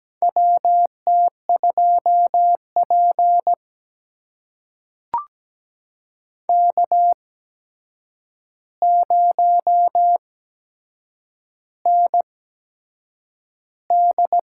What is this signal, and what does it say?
Morse code, heard as a keyed tone, about 17 wpm, spaced as for 7 wpm, 700 Hz. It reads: WT2P K0ND
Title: USA Call Signs - 4 Characters - 8x Character Spacing - 17wpm